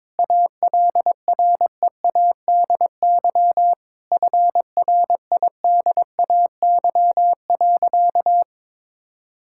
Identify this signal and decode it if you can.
Morse code, heard as a keyed tone, about 22 wpm, 700 Hz. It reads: ALREADY FRIDAY.
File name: Sentences with 7-Word Phrases - 22wpm